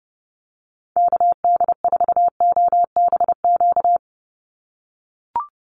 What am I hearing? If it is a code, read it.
Morse code, heard as a keyed tone, about 30 wpm, 700 Hz. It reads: KD4OBQ
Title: USA Call Signs 30wpm